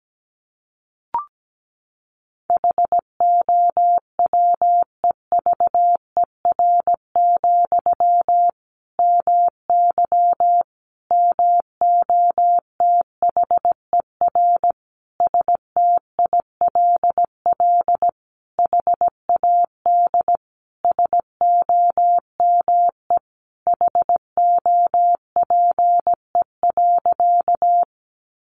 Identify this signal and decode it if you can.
Morse code, heard as a keyed tone, about 17 wpm, 700 Hz. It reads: HOWEVER, MY MOTHER STILL HAD SOME HOPE.
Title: Sentences from Top 1000 Words 17wpm